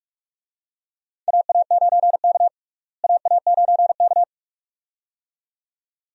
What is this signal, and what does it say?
Morse code, heard as a keyed tone, about 42 wpm, 700 Hz. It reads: AA9K AA9K
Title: USA Call Signs - 4 Characters - Repeated 2x - 45wpm